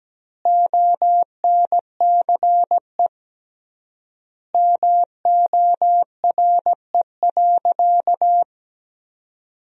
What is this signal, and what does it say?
Morse code, heard as a keyed tone, about 17 wpm, 700 Hz. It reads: ONCE MORE.
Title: 3x Word-Spacing - Sentences from Top 500 Words - 17wpm